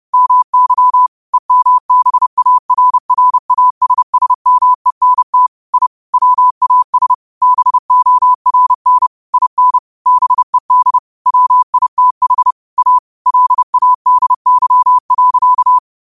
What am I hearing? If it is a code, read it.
Morse code, heard as a keyed tone, about 30 wpm, 1000 Hz. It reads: MY EMBARRASSMENT I WAS BORN IN BED WITH A LADY.